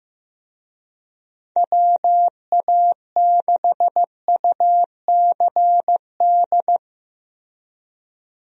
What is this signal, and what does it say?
Morse code, heard as a keyed tone, about 15 wpm, 700 Hz. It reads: WA6UCD
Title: USA Call Signs - 6 Characters - Repeated 3x - 15wpm